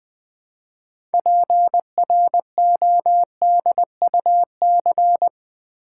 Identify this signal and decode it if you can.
Morse code, heard as a keyed tone, about 20 wpm, 700 Hz. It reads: PRODUC
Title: Build A Word - 3 Letters At A Time - 8-Letter Words - 20wpm